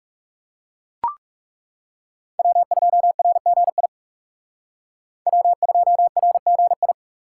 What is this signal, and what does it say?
Morse code, heard as a keyed tone, about 35 wpm, 700 Hz. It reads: W2RGI W2RGI
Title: Speed Racing - USA 1x3 Call Signs 30wpm